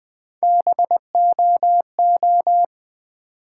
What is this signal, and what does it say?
Morse code, heard as a keyed tone, about 20 wpm, 700 Hz. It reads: BOO
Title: Build A Word - 1 Letter At A Time - 4-Letter Words - 20wpm